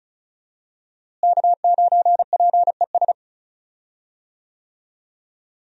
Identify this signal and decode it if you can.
Morse code, heard as a keyed tone, about 36 wpm, 700 Hz. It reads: K9PES
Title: USA Call Signs - 5 Characters - Repeated 2x - 35wpm